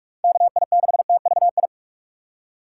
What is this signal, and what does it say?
Morse code, heard as a keyed tone, about 46 wpm, 700 Hz. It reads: KI6TVI
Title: USA Call Signs - 6x Character Spacing - 45wpm